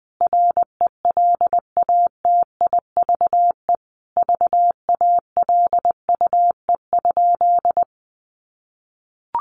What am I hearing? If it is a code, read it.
Morse code, heard as a keyed tone, about 20 wpm, 700 Hz. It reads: RELATIVE VALUE?
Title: Sentences from Top 1400 Words 20wpm